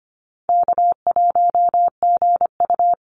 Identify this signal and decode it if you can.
Morse code, heard as a keyed tone, about 25 wpm, 700 Hz. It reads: K1GU
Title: Easy Call Signs Repeated 2x - 25wpm